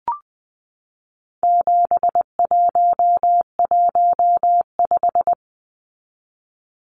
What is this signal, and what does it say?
Morse code, heard as a keyed tone, about 20 wpm, 700 Hz. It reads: 7115
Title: Numbers - 4-Digit 20wpm